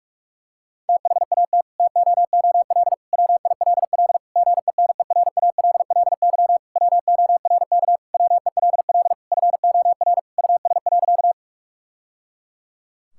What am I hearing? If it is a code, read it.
Morse code, heard as a keyed tone, about 46 wpm, 700 Hz. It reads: THAT TOOL WILL GENERALLY WORK WELL FOR US.
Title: Sentences from Top 1000 Words - Review Part 6 - 45wpm